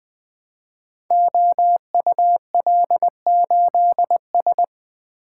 Morse code, heard as a keyed tone, about 20 wpm, 700 Hz. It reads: OUL8S
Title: Code Groups - 5 Letter:Numbers - 20wpm